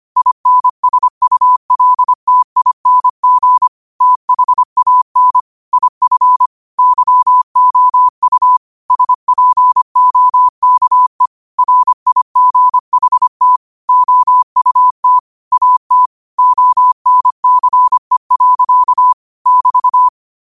Morse code, heard as a keyed tone, about 25 wpm, 1000 Hz. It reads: INSULTING THAN IF YOU SPOKE RIGHT OUT AT ONCE. <BT>
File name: qotdfiles / 25wpm